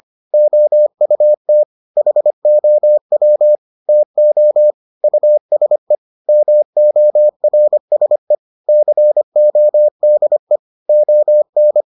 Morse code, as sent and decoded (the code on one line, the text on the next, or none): --- ..- - / .... --- .-- / - --- / ..- ... / -- --- .-. ... / -.-. --- -.. / --- -.
OUT HOW TO USE MORSE CODE ON